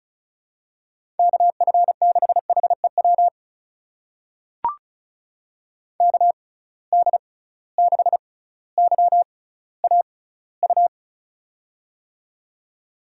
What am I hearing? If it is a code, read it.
Morse code, heard as a keyed tone, about 36 wpm, spaced as for 18 wpm, 700 Hz. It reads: KF6HEW KD6YAU
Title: USA Call Signs - 6 Characters - 6x Character Spacing - 35wpm